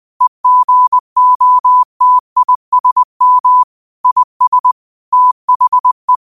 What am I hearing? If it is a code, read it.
Morse code, heard as a keyed tone, about 20 wpm, 1000 Hz. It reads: EGOTISM IS THE
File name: qotdfiles / 20wpm